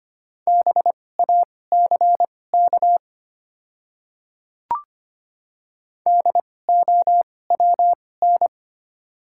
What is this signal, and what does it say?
Morse code, heard as a keyed tone, about 25 wpm, spaced as for 18 wpm, 690 Hz. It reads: BACK DOWN
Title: 2x Character Spacing - Top 200 Words - 25wpm